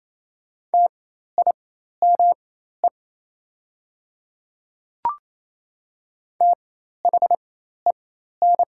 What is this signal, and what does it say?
Morse code, heard as a keyed tone, about 28 wpm, spaced as for 13 wpm, 700 Hz. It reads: TIME THEN
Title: 4x Character Spacing - Top 100 Words - 28wpm